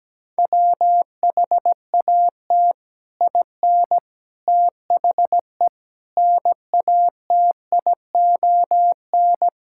Morse code, heard as a keyed tone, about 17 wpm, 700 Hz. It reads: WHAT IN THE NATION
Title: Sentences from Top 1000 Words - Review Part 6 - 17wpm